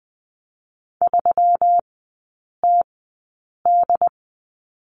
Morse code, heard as a keyed tone, about 20 wpm, spaced as for 8 wpm, 700 Hz. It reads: 3TD
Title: 127 Lesson 21e - Number 6 - Three Characters 20wpm